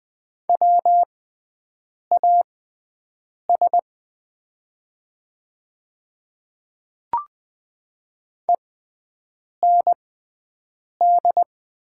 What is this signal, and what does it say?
Morse code, heard as a keyed tone, about 20 wpm, spaced as for 7 wpm, 700 Hz. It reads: WAS END
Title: 6x Character Spacing - Top 200 Words - 20wpm